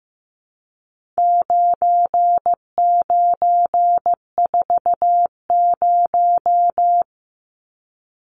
Morse code, heard as a keyed tone, about 15 wpm, 700 Hz. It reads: 9940
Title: Numbers - 4-Digit 15wpm